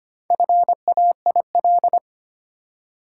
Morse code, heard as a keyed tone, about 25 wpm, 700 Hz. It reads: FAIL
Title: Four Letter Words - 25wpm